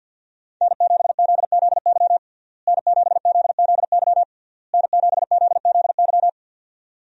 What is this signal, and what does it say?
Morse code, heard as a keyed tone, about 51 wpm, 700 Hz. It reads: N7ZZY N7ZZY N7ZZY
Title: USA Call Signs - 5 Characters - Repeated 3x - 50wpm